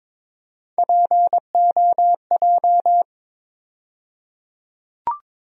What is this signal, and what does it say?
Morse code, heard as a keyed tone, about 22 wpm, 700 Hz. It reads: POJ
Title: USA Call Sign Postfixes - 3 Letters 22wpm